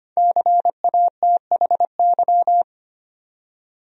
Morse code, heard as a keyed tone, about 25 wpm, 700 Hz. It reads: CATHY
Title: USA Common Names - 25wpm